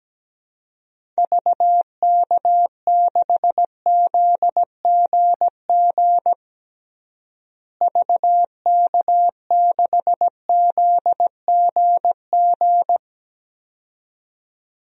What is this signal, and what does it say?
Morse code, heard as a keyed tone, about 17 wpm, 700 Hz. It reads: VK6ZGG VK6ZGG
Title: DXCC-100 Call Signs - 6 Characters - Repeated 2x - 17wpm